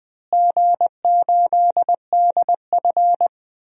Morse code, heard as a keyed tone, about 21 wpm, 700 Hz. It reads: G8DF
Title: DXCC-100 Call Signs - 4 Characters - Repeated 3x - 20wpm